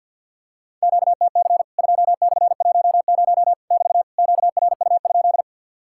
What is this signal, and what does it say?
Morse code, heard as a keyed tone, about 50 wpm, 700 Hz. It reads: QTC 2/10 <BT> QRU?
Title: Worked All Europe DX Contest - QTC Exchange with Repeats - 50wpm